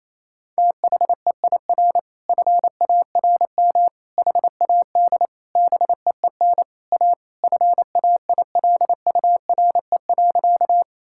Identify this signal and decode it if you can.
Morse code, heard as a keyed tone, about 28 wpm, 700 Hz. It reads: THEIR FARM HAD BEEN A FAILURE.
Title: Sentences from Top 1000 Words - Review Part 6 - 28wpm